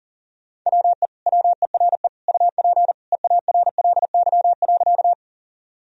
Morse code, heard as a keyed tone, about 40 wpm, 700 Hz. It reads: WE WERE UP EARLY.
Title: Sentences from Top 1000 Words - Review Part 2 - 40wpm